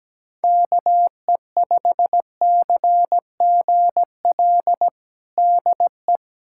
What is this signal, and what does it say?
Morse code, heard as a keyed tone, about 17 wpm, 700 Hz. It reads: KE5CGL DE
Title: USA Callsign DE Callsign - 6 Characters - 17wpm